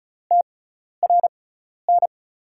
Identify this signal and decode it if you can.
Morse code, heard as a keyed tone, about 36 wpm, spaced as for 12 wpm, 700 Hz. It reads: TRN
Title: USA Call Signs - 6 Characters - 6x Character Spacing - 35wpm